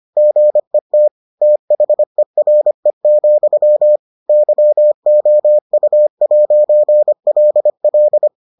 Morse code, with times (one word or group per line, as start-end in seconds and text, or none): 0.2-1.1 GET
1.4-4.0 THERE,
4.3-8.3 YOU'LL